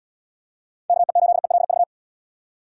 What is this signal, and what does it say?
Morse code, heard as a keyed tone, about 71 wpm, 700 Hz. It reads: KE8ELV